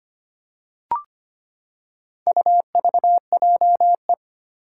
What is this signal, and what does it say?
Morse code, heard as a keyed tone, about 25 wpm, 700 Hz. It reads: UVJE
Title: Four Challenging Letter-Number Combinations 25wpm